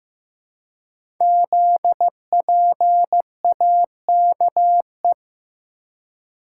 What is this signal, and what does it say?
Morse code, heard as a keyed tone, about 15 wpm, 700 Hz. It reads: ZPAKE
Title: Code Groups - 5 Letter:Numbers - 15wpm